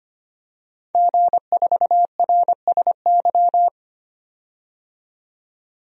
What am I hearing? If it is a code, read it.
Morse code, heard as a keyed tone, about 25 wpm, 700 Hz. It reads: G4RSY
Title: DXCC-100 Call Signs - 5 Characters - 25wpm